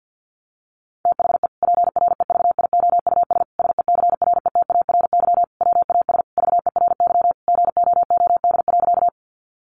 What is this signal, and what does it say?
Morse code, heard as a keyed tone, about 50 wpm, 700 Hz. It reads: THE PREVIOUS SECRETARY WAS VERY GOOD.
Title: Sentences from Top 1500 Words 50wpm